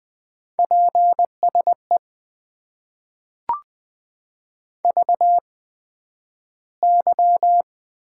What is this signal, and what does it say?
Morse code, heard as a keyed tone, about 20 wpm, spaced as for 11 wpm, 700 Hz. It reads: PSE VY
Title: Top 90 QSO Elements - 8x Character Spacing - 20wpm